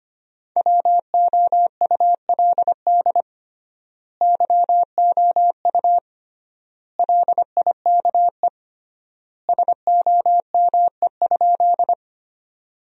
Morse code, heard as a keyed tone, about 25 wpm, 700 Hz. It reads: WOULD YOU LIKE SOME?
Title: 3x Word-Spacing - Sentences from Top 100 Words - 25wpm